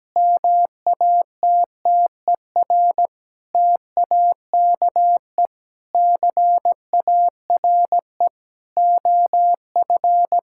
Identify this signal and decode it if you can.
Morse code, heard as a keyed tone, about 17 wpm, 700 Hz. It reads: MATTER TAKE CARE OF
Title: Sentences from Top 1000 Words - Review Part 5 - 17wpm